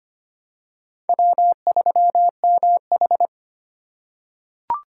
Morse code, heard as a keyed tone, about 25 wpm, 700 Hz. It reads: W3MH
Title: USA Call Signs - 4 Characters - 8x Character Spacing - 25wpm